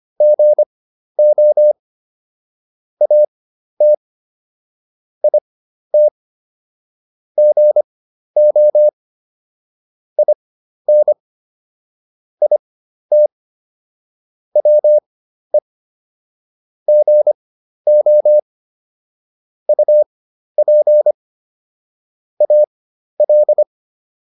Morse code, heard as a keyed tone, about 25 wpm, spaced as for 12 wpm, 600 Hz. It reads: GO AT IT GO IN IT WE GO UP AL